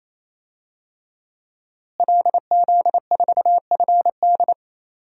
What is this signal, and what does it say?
Morse code, heard as a keyed tone, about 27 wpm, 700 Hz. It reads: LZ4FD
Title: DXCC-100 Call Signs - 5 Characters - Repeated 3x - 28wpm